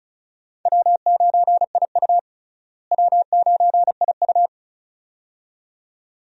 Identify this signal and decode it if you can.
Morse code, heard as a keyed tone, about 36 wpm, 700 Hz. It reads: W9IU W9IU
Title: USA Call Signs - 4 Characters - Repeated 2x - 35wpm